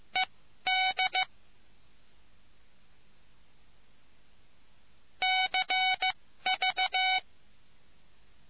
Morse code, heard as a keyed tone, about 15 wpm, spaced as for 12 wpm, 750 Hz. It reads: ED CV